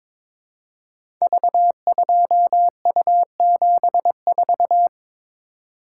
Morse code, heard as a keyed tone, about 22 wpm, 700 Hz. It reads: V2U74
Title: Code Groups - 5 Letter:Numbers - 22wpm